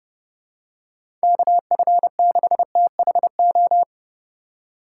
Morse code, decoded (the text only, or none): KF6THO